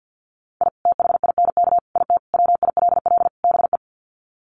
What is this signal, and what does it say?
Morse code, heard as a keyed tone, about 50 wpm, 700 Hz. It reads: I THINK IT WILL BE
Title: Sets of 5 Words Encore2 50wpm